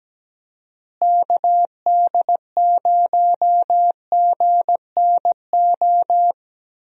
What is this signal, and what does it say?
Morse code, heard as a keyed tone, about 17 wpm, 700 Hz. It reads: KD0GNO
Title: USA Call Signs - 6 Characters - Repeated 2x - 17wpm